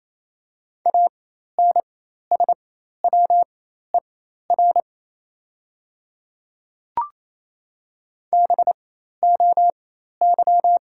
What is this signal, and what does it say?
Morse code, heard as a keyed tone, about 28 wpm, spaced as for 13 wpm, 700 Hz. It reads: ANSWER BOY
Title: 4x Character Spacing - Top 500 Words - 28wpm